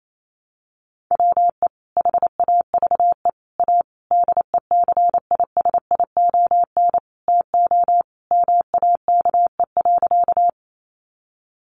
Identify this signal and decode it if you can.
Morse code, heard as a keyed tone, about 28 wpm, 700 Hz. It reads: WE HAVE A DECISION TO MAKE.
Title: Sentences from Top 1000 Words 28wpm